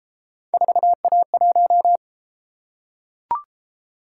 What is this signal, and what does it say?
Morse code, heard as a keyed tone, about 31 wpm, 700 Hz. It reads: 4A1